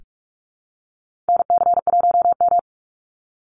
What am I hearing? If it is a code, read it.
Morse code, heard as a keyed tone, about 44 wpm, 700 Hz. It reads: NC1M
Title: Call Signs Repeated 2x - 45wpm